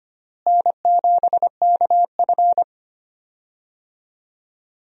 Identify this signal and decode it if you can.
Morse code, heard as a keyed tone, about 25 wpm, 700 Hz. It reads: N7KF
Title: USA Call Signs - 4 Characters - Repeated 3x - 25wpm